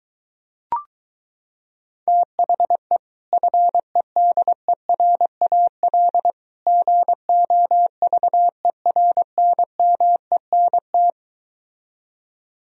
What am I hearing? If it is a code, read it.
Morse code, heard as a keyed tone, about 22 wpm, 700 Hz. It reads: THE FEDERAL GOVERNMENT